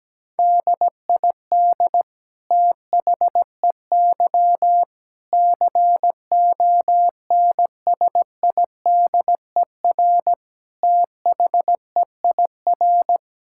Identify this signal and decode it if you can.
Morse code, heard as a keyed tone, about 17 wpm, 700 Hz. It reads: DID THEY CONSIDER THEIR
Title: Sentences from Top 1000 Words - Review Part 6 - 17wpm